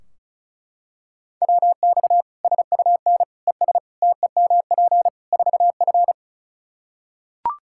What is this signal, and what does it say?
Morse code, heard as a keyed tone, about 34 wpm, 700 Hz. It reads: WX SUN ES TEMP 4F
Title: Practice QSO - Exchange 2 - Info - 35wpm